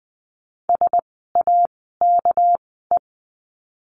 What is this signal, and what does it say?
Morse code, heard as a keyed tone, about 20 wpm, spaced as for 14 wpm, 700 Hz. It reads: SAKE
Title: 199 Lesson 32f - Letter k - Words 20wpm